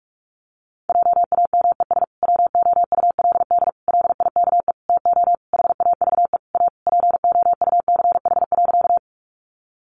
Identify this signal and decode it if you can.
Morse code, heard as a keyed tone, about 46 wpm, 700 Hz. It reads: JAMES WOULD LIKE TO HAVE A POUCH.
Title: Easy Phrases 45wpm